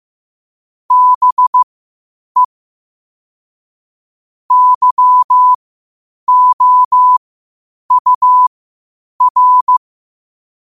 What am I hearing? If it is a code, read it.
Morse code, heard as a keyed tone, about 15 wpm, spaced as for 8 wpm, 1000 Hz. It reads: BE YOUR
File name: qotdfiles / 8wpm